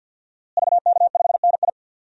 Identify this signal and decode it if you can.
Morse code, heard as a keyed tone, about 51 wpm, 700 Hz. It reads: VK5NI
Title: DXCC-100 Call Signs - 5 Characters - Repeated 2x - 50wpm